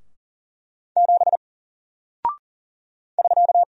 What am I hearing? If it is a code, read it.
Morse code, heard as a keyed tone, about 40 wpm, 700 Hz. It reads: 7 <SK>
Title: Single Character (Letter:Number:Punctuation:Prosign) Alt 40wpm